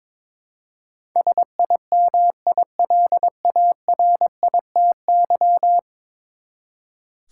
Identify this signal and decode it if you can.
Morse code, heard as a keyed tone, about 22 wpm, 700 Hz. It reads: SIMILARITY
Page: morse-code-ninja-practice-22wpm